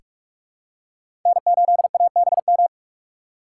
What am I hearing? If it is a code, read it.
Morse code, heard as a keyed tone, about 46 wpm, 700 Hz. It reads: N8ABM
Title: USA Call Signs - 2x Character Spacing - 45wpm